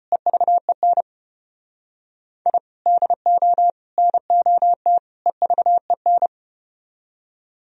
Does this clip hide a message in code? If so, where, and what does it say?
Yes: Morse code from 0.1 to 6.3 s, heard as a keyed tone, about 32 wpm, 700 Hz. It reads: EVEN I DO NOT EVEN